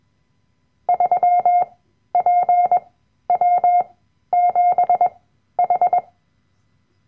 Morse code, heard as a keyed tone, about 21 wpm, spaced as for 12 wpm, 710 Hz. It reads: 3PW7H
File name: Workshop Dateien / CW-Online-Kurs / Lektion 010